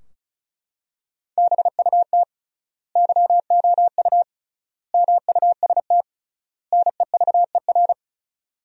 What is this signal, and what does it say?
Morse code, heard as a keyed tone, about 36 wpm, 700 Hz. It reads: BUT YOU MUST NEVER
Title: 3x Word-Spacing - Sentences from Top 300 Words - 35wpm